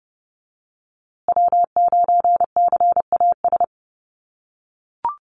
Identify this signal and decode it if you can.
Morse code, heard as a keyed tone, about 30 wpm, 700 Hz. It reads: W9CAS
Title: USA Call Signs Repeated 2x 30wpm